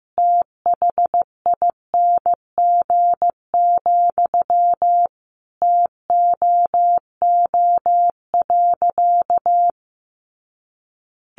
Morse code, heard as a keyed tone, about 15 wpm, 700 Hz. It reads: THING, TOO.
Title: Sentences from Top 200 Words 15wpm